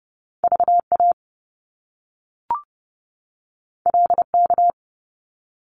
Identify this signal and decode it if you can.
Morse code, heard as a keyed tone, about 30 wpm, 700 Hz. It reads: VA LK
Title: Top Two Letter Combinations in Words 30wpm